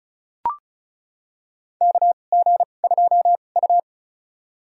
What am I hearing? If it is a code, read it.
Morse code, heard as a keyed tone, about 36 wpm, spaced as for 25 wpm, 700 Hz. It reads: KG2U
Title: USA Call Signs - 4 Characters - 2x Character Spacing - 35wpm